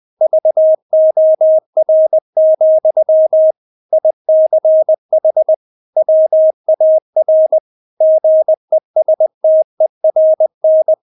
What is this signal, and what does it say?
Morse code, heard as a keyed tone, about 20 wpm, 630 Hz. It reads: VOR, ICH WAR GESTERN
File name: Witz1 20wpm